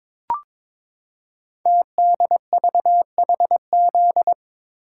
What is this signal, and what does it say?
Morse code, heard as a keyed tone, about 22 wpm, 700 Hz. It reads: TDVHZ
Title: Code Groups - 5 Letters - 22wpm